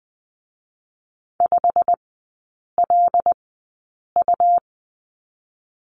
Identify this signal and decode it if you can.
Morse code, heard as a keyed tone, about 20 wpm, spaced as for 8 wpm, 700 Hz. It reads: 5LU
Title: 145 Lesson 24e - Letter y - Three Characters 20wpm